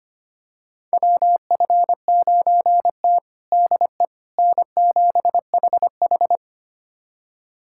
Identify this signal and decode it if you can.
Morse code, heard as a keyed tone, about 25 wpm, 700 Hz. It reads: WF9T DE N7HH
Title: USA Callsign DE Callsign - 4 Characters - 25wpm